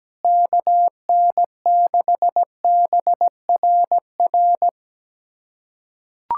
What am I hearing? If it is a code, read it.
Morse code, heard as a keyed tone, about 17 wpm, 700 Hz. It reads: KN6BRR
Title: USA Callsign DE Callsign 17wpm